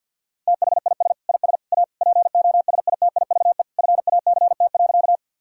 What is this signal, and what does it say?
Morse code, heard as a keyed tone, about 50 wpm, 700 Hz. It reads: THIS IS A POSITIVE FACT.
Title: Sentences from Top 1000 Words - Review Part 5 - 50wpm